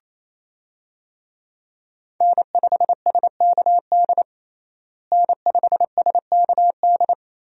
Morse code, heard as a keyed tone, about 28 wpm, 700 Hz. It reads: N5SKD N5SKD
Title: USA Call Signs - 5 Characters - Repeated 2x - 28wpm